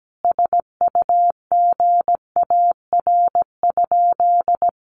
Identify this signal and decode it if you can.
Morse code, heard as a keyed tone, about 17 wpm, 700 Hz. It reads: SUGAR?
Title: Sentences from Top 1200 Words 17wpm